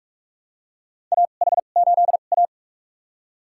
Morse code, heard as a keyed tone, about 46 wpm, spaced as for 33 wpm, 700 Hz. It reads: AH8A
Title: USA Call Signs - 2x Character Spacing - 45wpm